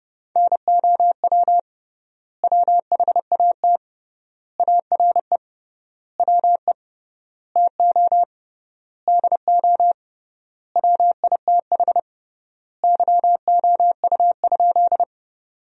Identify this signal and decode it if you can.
Morse code, heard as a keyed tone, about 30 wpm, 700 Hz. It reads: NOW WHAT ARE WE TO DO WITH YOU?
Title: 3x Word-Spacing - Sentences from Top 100 Words - 30wpm